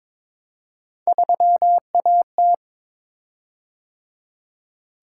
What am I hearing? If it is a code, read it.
Morse code, heard as a keyed tone, about 22 wpm, 700 Hz. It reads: 3AT
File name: practice / 22wpm / Three Letter-Number Combinations - 22wpm